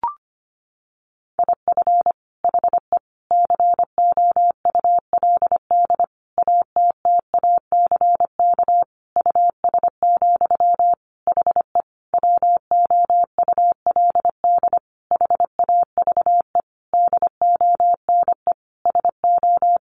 Morse code, heard as a keyed tone, about 25 wpm, 700 Hz. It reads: IF HE COULD ATTACK US, HE WOULD HAVE DONE SO
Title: Sentences from Top 1500 Words 25wpm